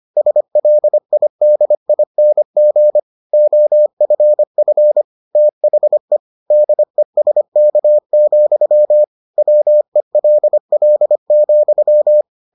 Morse code, heard as a keyed tone, about 25 wpm, 600 Hz. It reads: SLIDING OFF THE DESK, WELL,